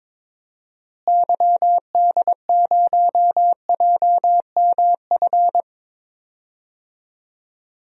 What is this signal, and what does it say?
Morse code, heard as a keyed tone, about 22 wpm, 700 Hz. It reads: YD0JMF
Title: DXCC-100 Call Signs - 6 Characters - Repeated 3x - 22wpm